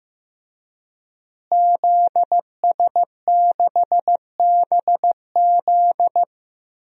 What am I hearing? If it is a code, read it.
Morse code, heard as a keyed tone, about 15 wpm, 700 Hz. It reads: ZS6BZ
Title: DXCC-100 Call Signs - 5 Characters - Repeated 2x - 15wpm